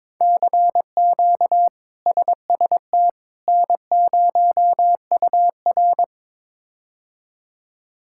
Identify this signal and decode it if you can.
Morse code, heard as a keyed tone, about 23 wpm, 700 Hz. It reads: CQ SST N0UR
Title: K1USN SST QSO Practice - 22wpm